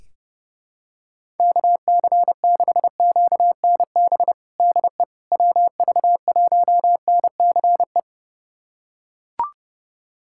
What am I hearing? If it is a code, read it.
Morse code, heard as a keyed tone, about 30 wpm, 700 Hz. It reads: KC6QNB DE WV1NCE
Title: USA Callsign DE Callsign - 6 Characters - 30wpm